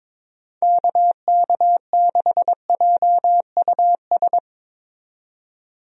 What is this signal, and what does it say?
Morse code, heard as a keyed tone, about 21 wpm, 700 Hz. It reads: KK6JUS
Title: USA Call Signs - 6 Characters - 22wpm